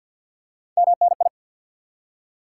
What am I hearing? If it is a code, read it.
Morse code, heard as a keyed tone, about 52 wpm, 700 Hz. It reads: MNI
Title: Top 90 QSO Elements 50wpm